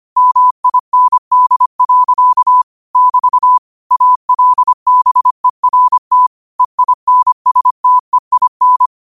Morse code, heard as a keyed tone, about 25 wpm, 1000 Hz. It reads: MIND. <BT> ALBERT EINSTEIN
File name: qotdfiles / 25wpm